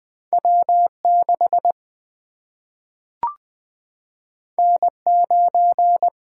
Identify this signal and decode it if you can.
Morse code, heard as a keyed tone, about 20 wpm, 700 Hz. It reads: W 6 N 9
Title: Single-Letter Single-Number 20wpm